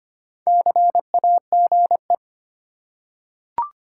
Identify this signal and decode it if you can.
Morse code, heard as a keyed tone, about 25 wpm, 690 Hz. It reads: CAGE
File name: practice / 25wpm / Easy Words with Hints - 25wpm